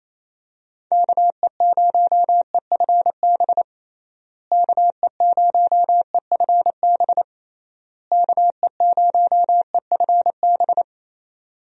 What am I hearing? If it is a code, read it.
Morse code, heard as a keyed tone, about 28 wpm, 700 Hz. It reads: KE0EFB KE0EFB KE0EFB
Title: USA Call Signs - 6 Characters - Repeated 3x - 28wpm